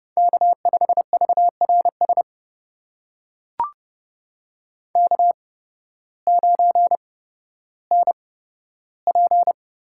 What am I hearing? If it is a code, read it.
Morse code, heard as a keyed tone, about 30 wpm, spaced as for 14 wpm, 700 Hz. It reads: K5VRS K9NP
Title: USA Call Signs - 5 Characters - 8x Character Spacing - 30wpm